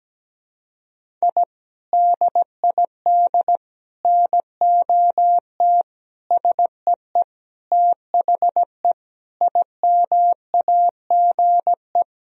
Morse code, heard as a keyed tone, about 17 wpm, 700 Hz. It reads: I DID NOT SEE THE IMAGE
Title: Sentences from Top 1000 Words - Review Part 4 - 17wpm